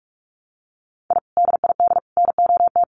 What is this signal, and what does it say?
Morse code, heard as a keyed tone, about 48 wpm, 700 Hz. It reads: I DID NOT
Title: Sets of 5 Words Encore 45wpm